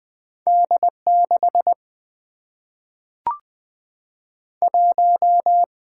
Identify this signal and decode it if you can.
Morse code, heard as a keyed tone, about 20 wpm, 700 Hz. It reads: D 6 1
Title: Single-Letter Single-Number 20wpm